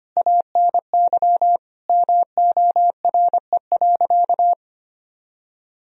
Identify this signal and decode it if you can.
Morse code, heard as a keyed tone, about 25 wpm, 700 Hz. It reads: ANY MORE.
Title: Sentences from Top 1000 Words - Review Part 4 - 25wpm